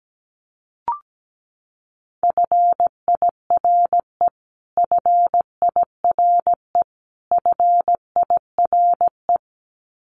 Morse code, heard as a keyed tone, about 17 wpm, 700 Hz. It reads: FIRE FIRE FIRE
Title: Top 500 Words Repeated 17wpm